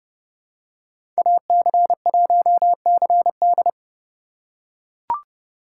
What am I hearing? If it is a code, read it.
Morse code, heard as a keyed tone, about 31 wpm, 700 Hz. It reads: AC1CD
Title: USA Call Signs - 5 Characters - 2x Character Spacing - 30wpm